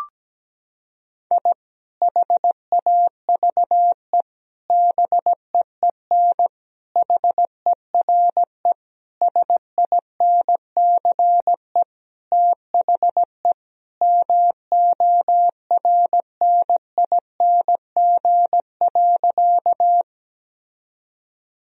Morse code, heard as a keyed tone, about 17 wpm, 700 Hz. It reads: I HAVE BEEN HERE SINCE THE MORNING.
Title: Sentences from Top 1000 Words - Review Part 1 - 17wpm